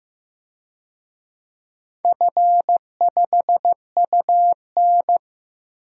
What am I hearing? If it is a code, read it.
Morse code, heard as a keyed tone, about 15 wpm, 700 Hz. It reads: F5UN
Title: DXCC-100 Call Signs - 4 Characters - Repeated 2x - 15wpm